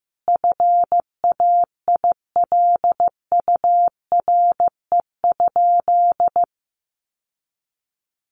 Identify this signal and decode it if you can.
Morse code, heard as a keyed tone, about 15 wpm, 700 Hz. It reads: FAILURE?
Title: Sentences from Top 1200 Words 15wpm